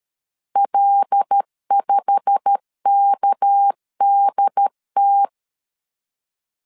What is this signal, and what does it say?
Morse code, heard as a keyed tone, about 13 wpm, 800 Hz. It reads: L5KDT
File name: LEZIONI / 9 2 12wpm 2301